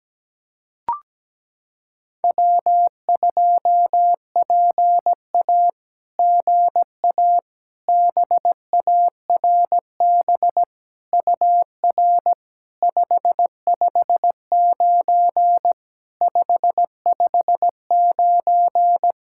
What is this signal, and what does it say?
Morse code, heard as a keyed tone, about 17 wpm, 700 Hz. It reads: W2PA GA BARB UR 559 559
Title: SOTA QSO Practice 17wpm